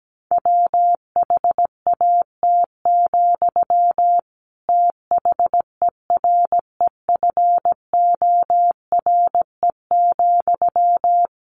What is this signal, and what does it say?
Morse code, heard as a keyed tone, about 17 wpm, 700 Hz. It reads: WHAT, THEREFORE,
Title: Sentences from Top 1100 Words 17wpm